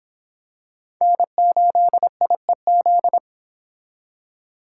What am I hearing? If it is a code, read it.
Morse code, heard as a keyed tone, about 26 wpm, 700 Hz. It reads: N8IEZ